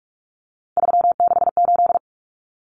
Morse code, heard as a keyed tone, about 44 wpm, 700 Hz. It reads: 368
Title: Numbers - 3-Digit 45wpm